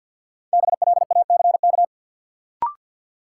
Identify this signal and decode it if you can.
Morse code, heard as a keyed tone, about 50 wpm, 700 Hz. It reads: BLACK
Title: Sets of 5 Words Encore - New Word Drill - 50wpm